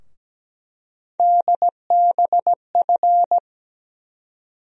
Morse code, heard as a keyed tone, about 17 wpm, 700 Hz. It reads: DBF
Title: USA Call Sign Postfixes - 3 Letters 17wpm